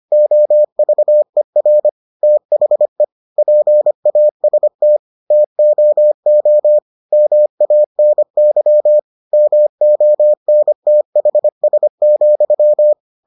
Morse code, heard as a keyed tone, about 25 wpm, 600 Hz. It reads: OVER THE PAST TOO MANY MONTHS,